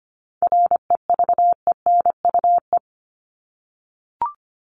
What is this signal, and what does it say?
Morse code, heard as a keyed tone, about 25 wpm, 700 Hz. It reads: REVENUE
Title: Top 1501-1750 Words 25wpm